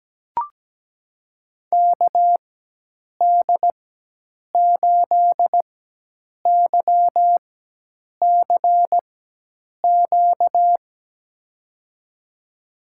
Morse code, heard as a keyed tone, about 17 wpm, spaced as for 8 wpm, 700 Hz. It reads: KD8YCQ
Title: USA Call Signs - 6 Characters - 4x Character Spacing - 17wpm